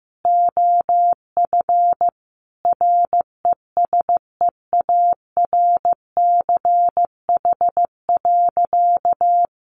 Morse code, heard as a keyed tone, about 15 wpm, 700 Hz. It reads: OF RESEARCH.
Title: Sentences from Top 1400 Words 15wpm